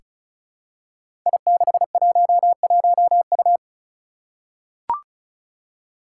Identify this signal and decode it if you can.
Morse code, heard as a keyed tone, about 36 wpm, 700 Hz. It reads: I611U
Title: Five Challenging Letter-Number Combinations 35wpm